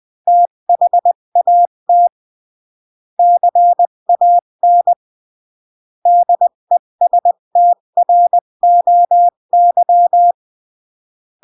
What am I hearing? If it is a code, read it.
Morse code, heard as a keyed tone, about 20 wpm, 700 Hz. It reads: THAT CAN DESTROY
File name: CW / jodorowsky / 82 commandments 20-15